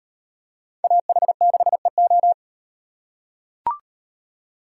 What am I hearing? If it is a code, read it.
Morse code, heard as a keyed tone, about 38 wpm, 700 Hz. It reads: AH6EO